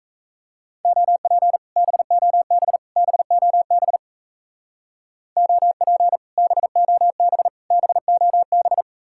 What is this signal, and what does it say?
Morse code, heard as a keyed tone, about 34 wpm, 700 Hz. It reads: OP BOB BOB OP BOB BOB
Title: Speed-Racing Common QSO Phrases - 28wpm